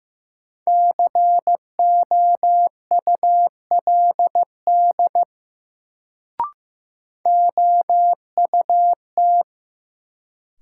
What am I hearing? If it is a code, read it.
Morse code, heard as a keyed tone, about 15 wpm, 700 Hz. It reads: COULD OUT
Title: Sentences from Top 100 Words - New Word Drill - 15wpm